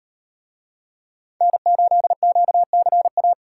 Morse code, heard as a keyed tone, about 38 wpm, 700 Hz. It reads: N8QCA